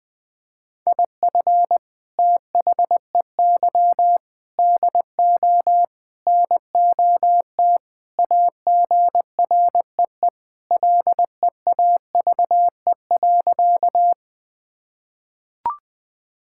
Morse code, heard as a keyed tone, about 20 wpm, 700 Hz. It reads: IF THEY DO NOT AGREE LEAVE.
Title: Sentences with 4-Word Phrases - Spoken Beginning - 20wpm